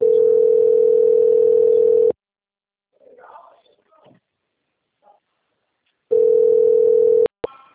Ringback tone (North America), 440 + 480 Hz.
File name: out- - - - - .67439